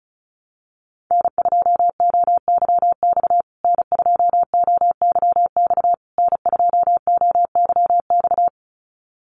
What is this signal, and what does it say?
Morse code, heard as a keyed tone, about 34 wpm, 700 Hz. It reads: N2OYX N2OYX N2OYX
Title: USA Call Signs Repeated 3x 35wpm